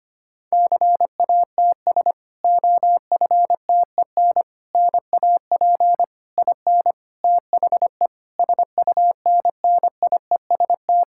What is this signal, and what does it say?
Morse code, heard as a keyed tone, about 25 wpm, 700 Hz. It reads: CATS OFTEN NAP IN THE SUNNIEST
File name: practice / 25wpm / Difficult Sentences with Hints - 25wpm